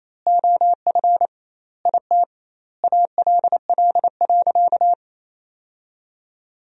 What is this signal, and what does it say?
Morse code, heard as a keyed tone, about 28 wpm, 700 Hz. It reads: OF IT ALL.
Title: 2x Word-Spacing - Sentences from Top 200 Words - 28wpm